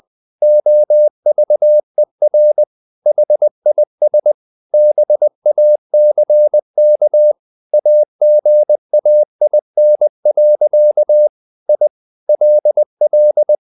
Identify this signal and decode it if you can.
Morse code, heard as a keyed tone, about 20 wpm, 600 Hz. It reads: OVER HIS BACK AGAIN. I LL
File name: the old man and the sea part 2 20wpm